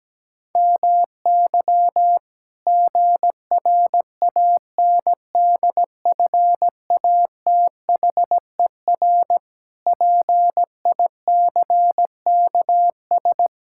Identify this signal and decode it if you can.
Morse code, heard as a keyed tone, about 17 wpm, 700 Hz. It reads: MY GRANDFATHER PICKS
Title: Pangram Challenge v2 - 17wpm